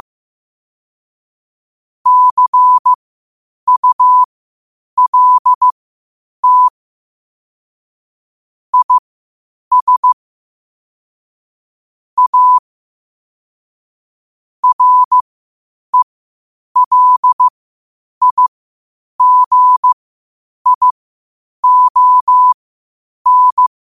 Morse code, heard as a keyed tone, about 15 wpm, spaced as for 8 wpm, 1000 Hz. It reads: CULT IS A RELIGION
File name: qotdfiles / 8wpm